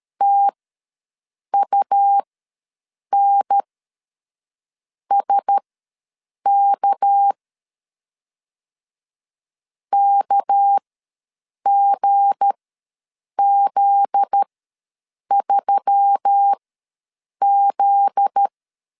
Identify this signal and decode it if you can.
Morse code, heard as a keyed tone, about 13 wpm, spaced as for 7 wpm, 800 Hz. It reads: TUNSK KGZ3Z
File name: 7 GF9Z ws 12wpm